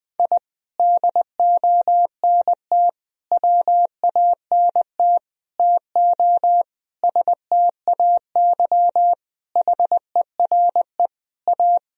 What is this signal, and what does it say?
Morse code, heard as a keyed tone, about 20 wpm, 700 Hz. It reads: I DONT WANT TO STAY HERE A
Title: Sentences from Top 1000 Words - Review Part 4 - 20wpm